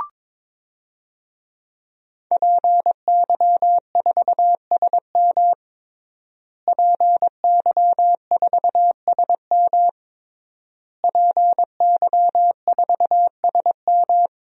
Morse code, heard as a keyed tone, about 22 wpm, 700 Hz. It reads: PY4SM PY4SM PY4SM
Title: DXCC-100 Call Signs - 5 Characters - Repeated 3x - 22wpm